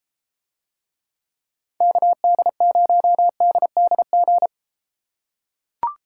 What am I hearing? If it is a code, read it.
Morse code, heard as a keyed tone, about 32 wpm, 700 Hz. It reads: KD0DDG